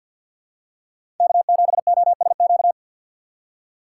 Morse code, heard as a keyed tone, about 54 wpm, 700 Hz. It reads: K7OSQ
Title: USA Call Signs - 5 Characters - 8x Character Spacing - 50wpm